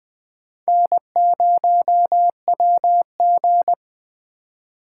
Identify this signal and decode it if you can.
Morse code, heard as a keyed tone, about 21 wpm, 700 Hz. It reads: N0WG
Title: Speed Racing - USA 1x2 Call Signs 17wpm